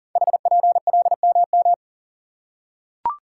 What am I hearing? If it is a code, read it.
Morse code, heard as a keyed tone, about 39 wpm, 700 Hz. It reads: HPLMM
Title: Code Groups - 5 Letters - 40wpm